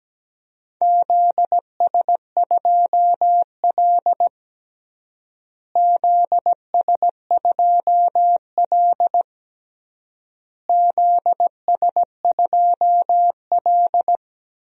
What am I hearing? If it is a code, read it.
Morse code, heard as a keyed tone, about 17 wpm, 700 Hz. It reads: ZS2L ZS2L ZS2L
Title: DXCC-100 Call Signs - 4 Characters - Repeated 3x - 17wpm